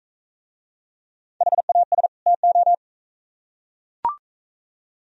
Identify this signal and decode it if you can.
Morse code, heard as a keyed tone, about 38 wpm, 700 Hz. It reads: HAS TO